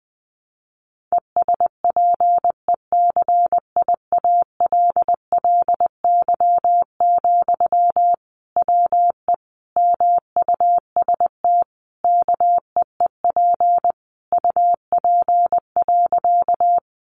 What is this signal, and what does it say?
Morse code, heard as a keyed tone, about 20 wpm, 700 Hz. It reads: ESPECIALLY, WE MUST KEEP UP.
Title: Sentences from Top 700 Words 20wpm